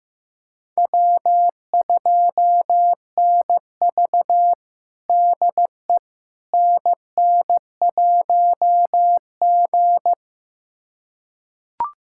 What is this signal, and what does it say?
Morse code, heard as a keyed tone, about 15 wpm, 700 Hz. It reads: W2NV DE NN1G
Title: USA Callsign DE Callsign - 4 Characters - 15wpm